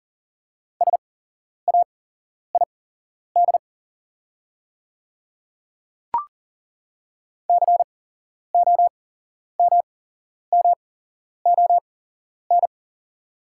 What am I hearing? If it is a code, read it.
Morse code, heard as a keyed tone, about 40 wpm, spaced as for 11 wpm, 690 Hz. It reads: SAID COMMON
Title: 8x Character Spacing - Top 500 Words - 40wpm